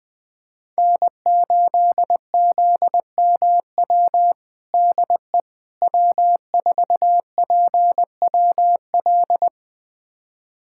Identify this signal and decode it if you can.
Morse code, heard as a keyed tone, about 20 wpm, 700 Hz. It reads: N8ZMW DE W4PWL
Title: USA Callsign DE Callsign - 5 Characters - 20wpm